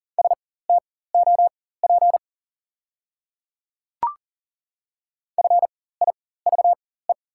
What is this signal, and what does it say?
Morse code, heard as a keyed tone, about 41 wpm, spaced as for 19 wpm, 690 Hz. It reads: STOP FIVE
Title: 4x Character Spacing - Top 500 Words - 40wpm